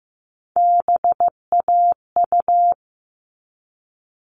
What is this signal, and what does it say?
Morse code, heard as a keyed tone, about 15 wpm, 700 Hz. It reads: BAU
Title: All Cut Numbers - 3-Digit 15wpm